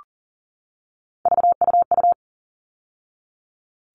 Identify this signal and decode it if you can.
Morse code, heard as a keyed tone, about 40 wpm, 700 Hz. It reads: VUU
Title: All Cut Numbers - 3-Digit 40wpm